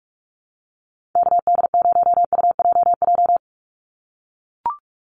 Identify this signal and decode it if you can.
Morse code, heard as a keyed tone, about 44 wpm, 690 Hz. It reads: KD0UJJ
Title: USA Call Signs 45wpm